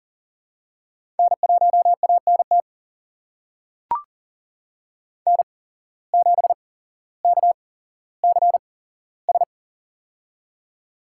Morse code, heard as a keyed tone, about 38 wpm, spaced as for 17 wpm, 700 Hz. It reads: N1ANT N7KCS
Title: USA Call Signs - 5 Characters - 8x Character Spacing - 40wpm